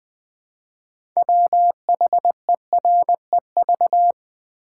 Morse code, heard as a keyed tone, about 20 wpm, 690 Hz. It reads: WHEREV
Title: Build A Word - 3 Letters At A Time - 8-Letter Words - 20wpm